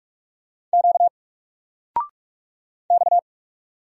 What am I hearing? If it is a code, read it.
Morse code, heard as a keyed tone, about 44 wpm, 700 Hz. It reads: Q X
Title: Single Letter-Number 45wpm